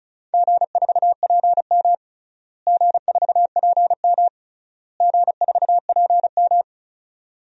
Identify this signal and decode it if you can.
Morse code, heard as a keyed tone, about 34 wpm, 700 Hz. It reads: G4PM G4PM G4PM
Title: DXCC-100 Call Signs - 4 Characters - Repeated 3x - 35wpm